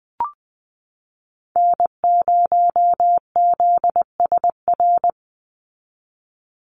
Morse code, heard as a keyed tone, about 20 wpm, 700 Hz. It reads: N0ZSR
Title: USA Call Signs 20wpm